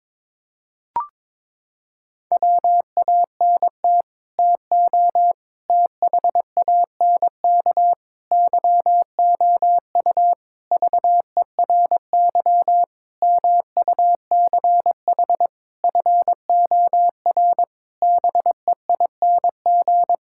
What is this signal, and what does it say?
Morse code, heard as a keyed tone, about 22 wpm, 700 Hz. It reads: WANT TO THANK YOU VERY MUCH FOR BEING
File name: Sets of 9 Words - 22wpm